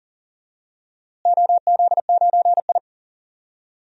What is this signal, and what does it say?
Morse code, heard as a keyed tone, about 40 wpm, 700 Hz. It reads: OZ9I
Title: DXCC-100 Call Signs - 4 Characters - Repeated 3x - 40wpm